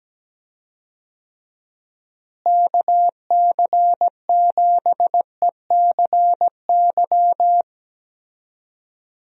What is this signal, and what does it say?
Morse code, heard as a keyed tone, about 17 wpm, 700 Hz. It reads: KC7ECY
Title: USA Call Signs - 6 Characters - Repeated 3x - 17wpm